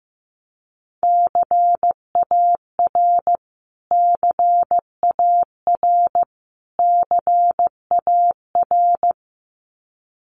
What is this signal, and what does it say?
Morse code, heard as a keyed tone, about 15 wpm, 700 Hz. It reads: CAR CAR CAR
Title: Top 500 Words Repeated 15wpm